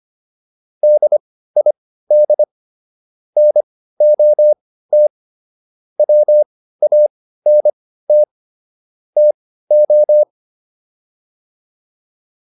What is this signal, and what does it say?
Morse code, heard as a keyed tone, about 25 wpm, spaced as for 15 wpm, 600 Hz. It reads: DID NOT WANT TO